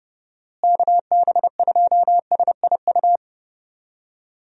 Morse code, heard as a keyed tone, about 32 wpm, 700 Hz. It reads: KB2SIU
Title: USA Call Signs - 6 Characters - Repeated 2x - 30wpm